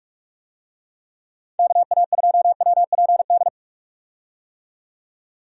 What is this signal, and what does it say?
Morse code, heard as a keyed tone, about 46 wpm, 700 Hz. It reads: KA2WPD
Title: USA Call Signs - 6 Characters - 45wpm